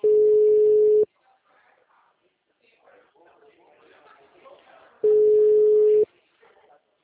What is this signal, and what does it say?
Ringback tone (Europe, 425 Hz), with speech.
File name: out- -2006- - - .39879